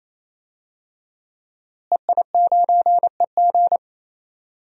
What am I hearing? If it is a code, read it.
Morse code, heard as a keyed tone, about 28 wpm, 700 Hz. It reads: EI9EG
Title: DXCC-100 Call Signs - 5 Characters - Repeated 3x - 28wpm